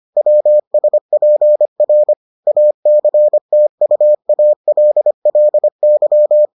Morse code, heard as a keyed tone, about 25 wpm, 600 Hz. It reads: WSPR ACTUALLY